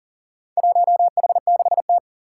Morse code, heard as a keyed tone, about 38 wpm, 700 Hz. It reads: 1H6T
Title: Four Challenging Letter-Number Combinations 40wpm